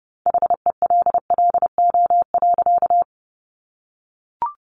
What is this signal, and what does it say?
Morse code, heard as a keyed tone, about 30 wpm, 700 Hz. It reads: HELLO.
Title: Sentences from Top 1100 Words 30wpm